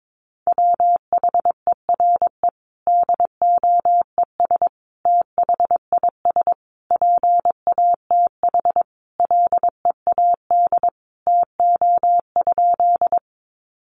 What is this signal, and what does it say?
Morse code, heard as a keyed tone, about 22 wpm, 700 Hz. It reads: WHERE DOES THIS PATH LEAD TO?
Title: Sentences from Top 1400 Words 22wpm